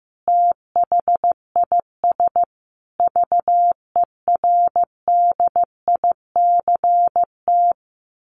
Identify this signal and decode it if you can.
Morse code, heard as a keyed tone, about 15 wpm, 700 Hz. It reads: THIS VERDICT
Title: A Sound Of Thunder 15wpm